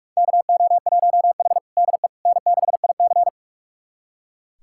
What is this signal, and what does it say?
Morse code, heard as a keyed tone, about 44 wpm, 690 Hz. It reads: KO1H DE N6IC
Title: USA Callsign DE Callsign - 4 Characters - 45wpm